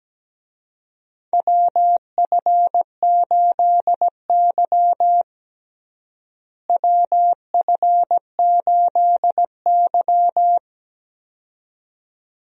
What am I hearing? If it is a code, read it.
Morse code, heard as a keyed tone, about 17 wpm, 700 Hz. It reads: WF8Y WF8Y
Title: USA Call Signs - 4 Characters - Repeated 2x - 17wpm